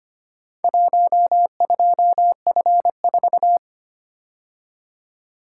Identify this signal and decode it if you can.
Morse code, heard as a keyed tone, about 25 wpm, 700 Hz. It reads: 12F4
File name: practice / 25wpm / Four Letter-Number Combinations - 25wpm